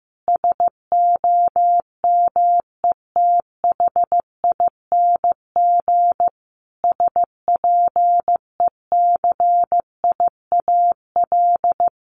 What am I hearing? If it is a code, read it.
Morse code, heard as a keyed tone, about 15 wpm, 700 Hz. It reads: SOMETHING SPECIAL
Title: Sentences from Top 1500 Words 15wpm